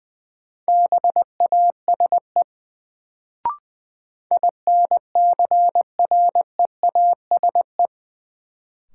Morse code, heard as a keyed tone, about 20 wpm, 700 Hz. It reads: BASE INCREASE
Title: Sets of 5 Words - New Word Drill - 20wpm